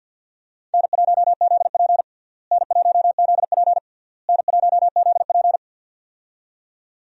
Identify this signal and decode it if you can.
Morse code, heard as a keyed tone, about 48 wpm, 700 Hz. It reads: N1ZP N1ZP N1ZP
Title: USA Call Signs - 4 Characters - Repeated 3x - 50wpm